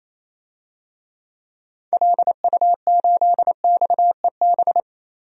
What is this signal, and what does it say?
Morse code, heard as a keyed tone, about 28 wpm, 700 Hz. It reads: LU8XEB
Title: DXCC-100 Call Signs - 6 Characters - Repeated 2x - 28wpm